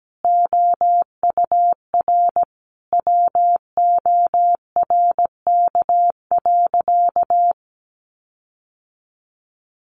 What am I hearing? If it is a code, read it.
Morse code, heard as a keyed tone, about 17 wpm, 700 Hz. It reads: OUR WORK.
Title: Sentences from Top 1300 Words 17wpm